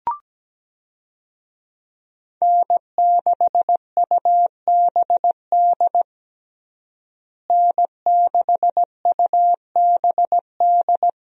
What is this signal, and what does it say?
Morse code, heard as a keyed tone, about 17 wpm, 700 Hz. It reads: N6UBD N6UBD
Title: USA Call Signs - 5 Characters - Repeated 3x - 17wpm